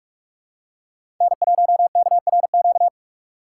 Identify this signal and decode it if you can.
Morse code, heard as a keyed tone, about 46 wpm, 700 Hz. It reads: N1KRQ
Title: USA Call Signs - 5 Characters - 3x Character Spacing - 45wpm